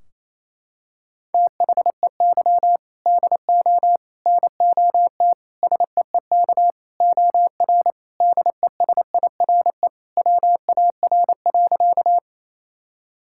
Morse code, heard as a keyed tone, about 28 wpm, 700 Hz. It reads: THEY DO NOT SEEK OR DESIRE WAR.
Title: Sentences from Top 1600 Words 28wpm